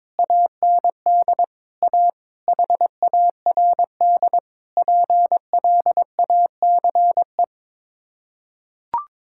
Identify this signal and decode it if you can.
Morse code, heard as a keyed tone, about 22 wpm, 700 Hz. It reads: AND A HARD PLACE
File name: practice / 22wpm / Sets of 8 Words - 22wpm